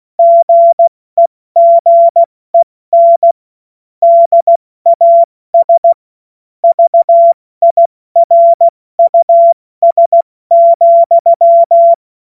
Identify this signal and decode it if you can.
Morse code, heard as a keyed tone, about 16 wpm, 680 Hz. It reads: GEGEN DAS VIRUS,
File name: Morsekurs / Witze / Witz 04. Mai 2021, 70 F80